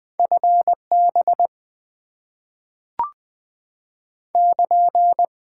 Morse code, heard as a keyed tone, about 20 wpm, 700 Hz. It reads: FB <KN>
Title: Top 90 QSO Elements - 6x Character Spacing - 20wpm